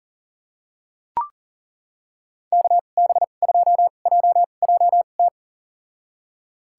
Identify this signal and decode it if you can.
Morse code, heard as a keyed tone, about 40 wpm, spaced as for 29 wpm, 700 Hz. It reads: KB2JJT
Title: USA Call Signs - 6 Characters - 2x Character Spacing - 40wpm